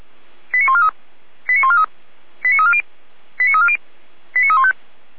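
Five-tone selective call (ZVEI), heard as quick runs of five tones, tones of about 70 ms. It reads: 89143 89143 89340 89340 89315
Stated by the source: A ZVEI five-tone selective call received by SDR.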